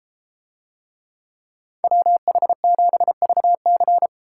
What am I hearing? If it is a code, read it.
Morse code, heard as a keyed tone, about 32 wpm, 700 Hz. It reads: WH7VC